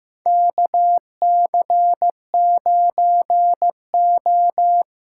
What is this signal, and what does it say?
Morse code, heard as a keyed tone, about 15 wpm, 700 Hz. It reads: KC9O
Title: USA Call Signs - 6 Characters - 8x Character Spacing - 15wpm